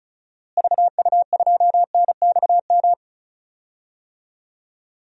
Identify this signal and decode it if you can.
Morse code, heard as a keyed tone, about 36 wpm, 700 Hz. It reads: VU2NXM
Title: DXCC-100 Call Signs - 6 Characters - Repeated 3x - 35wpm